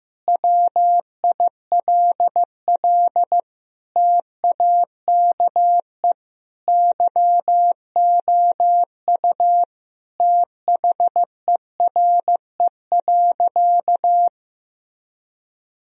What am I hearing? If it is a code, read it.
Morse code, heard as a keyed tone, about 15 wpm, 700 Hz. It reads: WILL TAKE YOU THERE.
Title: Sentences from Top 1000 Words - Review Part 3 - 15wpm